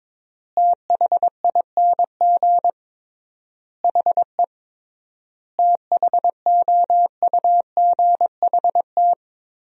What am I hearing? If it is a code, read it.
Morse code, heard as a keyed tone, about 22 wpm, 700 Hz. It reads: THING HE THOUGHT
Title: 3x Word-Spacing - Sentences from Top 300 Words - 22wpm